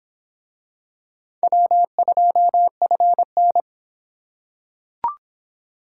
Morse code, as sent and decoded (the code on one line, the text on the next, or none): .-- ..--- ..-. -.
W2FN